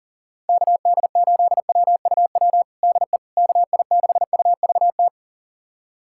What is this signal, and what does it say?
Morse code, heard as a keyed tone, about 40 wpm, 700 Hz. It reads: KD8WUW DE KI6UVT
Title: USA Callsign DE Callsign - 6 Characters - 40wpm